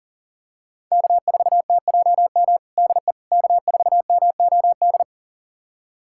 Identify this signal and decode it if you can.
Morse code, heard as a keyed tone, about 40 wpm, 700 Hz. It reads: K4TJM DE K4MOD
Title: USA Callsign DE Callsign - 5 Characters - 40wpm